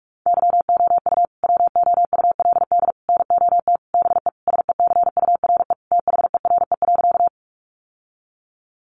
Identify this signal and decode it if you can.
Morse code, heard as a keyed tone, about 44 wpm, 700 Hz. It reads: YOU WOULD NOT BE SECURE THERE.
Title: Sentences from Top 1200 Words 45wpm